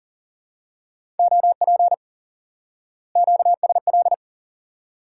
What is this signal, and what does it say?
Morse code, heard as a keyed tone, about 40 wpm, 700 Hz. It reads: OP QSL
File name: IQE Course 3 - Lesson 89 - YAGI - Two Elements - 40wpm